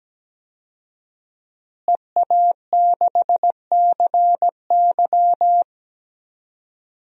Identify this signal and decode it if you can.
Morse code, heard as a keyed tone, about 17 wpm, 700 Hz. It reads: EA6CY
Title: DXCC-100 Call Signs - 5 Characters - Repeated 2x - 17wpm